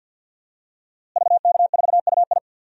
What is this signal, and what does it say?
Morse code, heard as a keyed tone, about 52 wpm, 700 Hz. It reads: VK4UI
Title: DXCC-100 Call Signs - 5 Characters - Repeated 2x - 50wpm